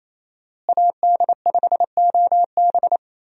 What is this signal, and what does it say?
Morse code, heard as a keyed tone, about 28 wpm, 700 Hz. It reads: AD5OB
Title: USA Call Signs - 5 Characters - Repeated 2x - 28wpm